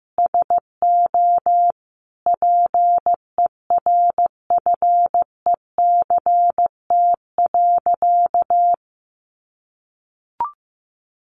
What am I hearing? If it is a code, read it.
Morse code, heard as a keyed tone, about 15 wpm, 700 Hz. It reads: SO PERFECT.
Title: Sentences from Top 1300 Words 15wpm